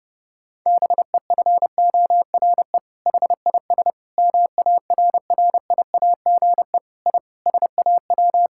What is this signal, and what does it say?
Morse code, heard as a keyed tone, about 32 wpm, 700 Hz. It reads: BEFORE HIS MARRIAGE I SAW
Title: Sentences from Top 1000 Words - Review Part 5 - 30wpm